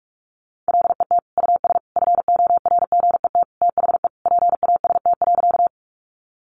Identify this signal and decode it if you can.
Morse code, heard as a keyed tone, about 44 wpm, 700 Hz. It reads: LET US FORGET THE PAST.
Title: Sentences from Top 1200 Words 45wpm